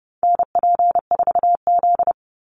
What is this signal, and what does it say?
Morse code, heard as a keyed tone, about 30 wpm, 700 Hz. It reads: NP4Z
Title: Call Signs Repeated 2x - 30wpm